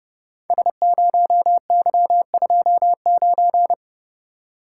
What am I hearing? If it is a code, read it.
Morse code, heard as a keyed tone, about 30 wpm, 700 Hz. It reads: S0Y29
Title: Code Groups - 5 Letter:Numbers - 30wpm